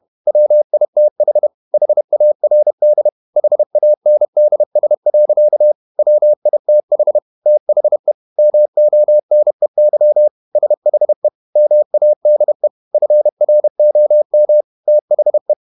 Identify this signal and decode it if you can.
Morse code, heard as a keyed tone, about 31 wpm, 600 Hz. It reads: WITH HARD HANDS. WITH THE MONEY SHE MADE FROM THE
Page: Never 31wpm | CW med Gnister